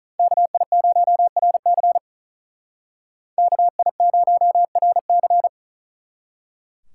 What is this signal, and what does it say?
Morse code, heard as a keyed tone, about 46 wpm, 700 Hz. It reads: KI0RC KI0RC